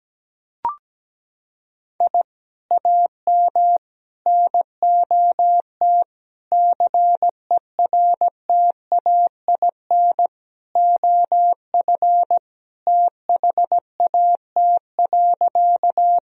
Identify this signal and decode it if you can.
Morse code, heard as a keyed tone, about 17 wpm, 700 Hz. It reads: I AM NOT CERTAIN OF THAT.
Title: Sentences from Top 1000 Words - Review Part 2 - 17wpm